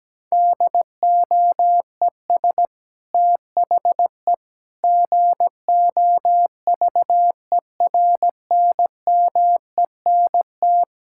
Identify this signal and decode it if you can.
Morse code, heard as a keyed tone, about 17 wpm, 700 Hz. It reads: DOES THE GOVERNMENT
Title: Sentences from Top 1600 Words 17wpm